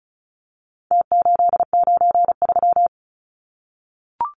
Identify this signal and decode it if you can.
Morse code, heard as a keyed tone, about 34 wpm, 700 Hz. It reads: T893
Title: Numbers - Contest Style - 4-Digit 35wpm